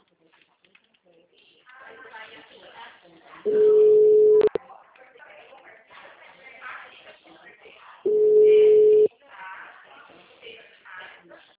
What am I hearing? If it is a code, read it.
Ringback tone (Europe, 425 Hz), with speech.